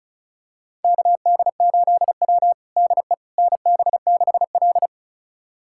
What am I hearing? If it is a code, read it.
Morse code, heard as a keyed tone, about 36 wpm, 700 Hz. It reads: KD8W DE NB6L
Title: USA Callsign DE Callsign - 4 Characters - 35wpm